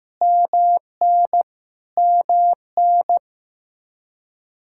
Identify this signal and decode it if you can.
Morse code, heard as a keyed tone, about 15 wpm, 700 Hz. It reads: MN MN
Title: Practice QSO - Exchange 1 - Beginning - 15wpm